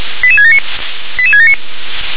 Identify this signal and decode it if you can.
Five-tone selective call (ZVEI), heard as quick runs of five tones, tones of about 70 ms. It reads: 88679 88679
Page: SDR/ZVEI